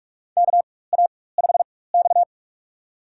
Morse code, heard as a keyed tone, about 46 wpm, spaced as for 21 wpm, 700 Hz. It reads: KA5X
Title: USA Call Signs - 4 Characters - 4x Character Spacing - 45wpm